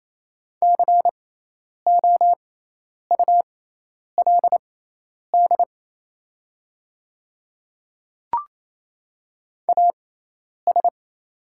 Morse code, heard as a keyed tone, about 27 wpm, spaced as for 10 wpm, 690 Hz. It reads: COULD AS
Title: 6x Character Spacing - Top 300 Words - 28wpm